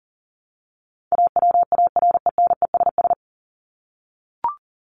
Morse code, heard as a keyed tone, about 40 wpm, 700 Hz. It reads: AWARENESS
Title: Top 1501-1750 Words 40wpm